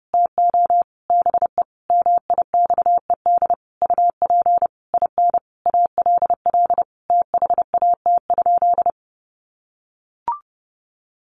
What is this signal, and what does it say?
Morse code, heard as a keyed tone, about 30 wpm, 700 Hz. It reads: TO BE MIXED UP IN ALL THAT?
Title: Sentences from Top 1500 Words 30wpm